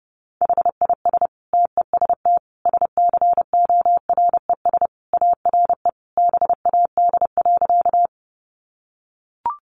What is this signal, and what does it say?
Morse code, heard as a keyed tone, about 30 wpm, 700 Hz. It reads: HIS TEST SCORES ARE BAD.
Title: Additional Easy Phrases 30wpm